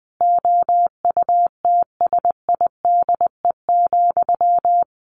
Morse code, heard as a keyed tone, about 20 wpm, 700 Hz. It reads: OUTSIDE,
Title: Sentences from Top 900 Words 20wpm